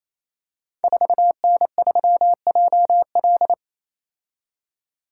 Morse code, heard as a keyed tone, about 28 wpm, 700 Hz. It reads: 4N3JL
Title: Code Groups - 5 Letter:Numbers - 28wpm